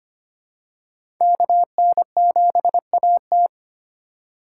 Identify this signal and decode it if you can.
Morse code, heard as a keyed tone, about 25 wpm, 700 Hz. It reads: KN7AT
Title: USA Call Signs - 5 Characters - 3x Character Spacing - 25wpm